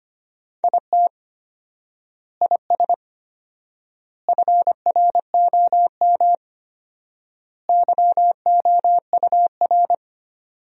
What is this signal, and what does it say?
Morse code, heard as a keyed tone, about 25 wpm, 700 Hz. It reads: IT IS FROM YOUR
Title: 4x Word-Spacing - Sentences from Top 300 Words - 25wpm